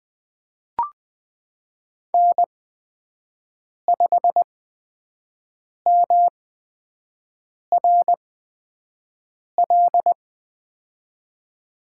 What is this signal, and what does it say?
Morse code, heard as a keyed tone, about 20 wpm, spaced as for 5 wpm, 700 Hz. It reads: N5MRL
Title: USA Call Signs - 5 Characters - 8x Character Spacing - 20wpm